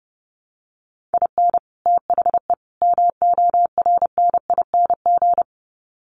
Morse code, heard as a keyed tone, about 30 wpm, 700 Hz. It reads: IN THE MORNING
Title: Sets of 3 Words 30wpm